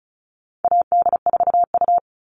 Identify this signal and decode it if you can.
Morse code, heard as a keyed tone, about 34 wpm, 700 Hz. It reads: AD4U
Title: All Cut Numbers - 4-Digit 35wpm